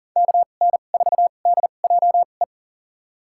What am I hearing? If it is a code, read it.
Morse code, heard as a keyed tone, about 40 wpm, spaced as for 29 wpm, 700 Hz. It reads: KN4DJE
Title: USA Call Signs - 6 Characters - 2x Character Spacing - 40wpm